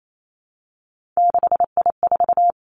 Morse code, heard as a keyed tone, about 28 wpm, 700 Hz. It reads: 6I4
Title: Three Challenging Letter-Number Combinations 28wpm